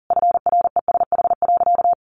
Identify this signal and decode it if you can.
Morse code, heard as a keyed tone, about 40 wpm, 700 Hz. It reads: FRESH.
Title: Sentences from Top 1200 Words 40wpm